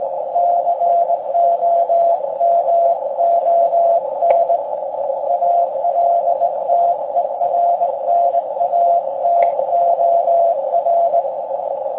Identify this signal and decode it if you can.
Morse code, heard as a keyed tone, about 19 wpm, 690 Hz, with noise. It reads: COMON ALTERNATOR
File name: forum / im / aq grimeton